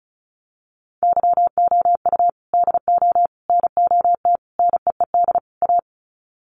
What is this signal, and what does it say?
Morse code, heard as a keyed tone, about 34 wpm, 700 Hz. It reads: YOU DO NOT NEED A
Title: Sets of 5 Words Encore 35wpm